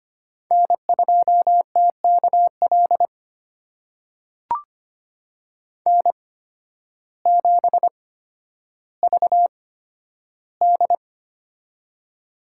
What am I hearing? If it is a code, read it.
Morse code, heard as a keyed tone, about 25 wpm, spaced as for 12 wpm, 700 Hz. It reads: N2TKL N7VD
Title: USA Call Signs - 8x Character Spacing - 25wpm